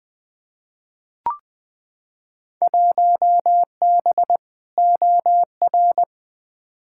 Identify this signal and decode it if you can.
Morse code, heard as a keyed tone, about 21 wpm, 700 Hz. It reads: 1B OR
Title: Field Day Exchanges - 20wpm